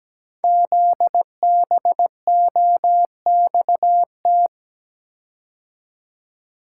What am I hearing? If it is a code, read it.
Morse code, heard as a keyed tone, about 17 wpm, 700 Hz. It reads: ZBOXT
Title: Code Groups - 5 Letters - 17wpm